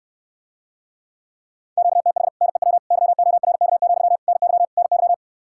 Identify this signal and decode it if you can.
Morse code, heard as a keyed tone, about 69 wpm, 700 Hz. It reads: QTH NEW YORK, NY NY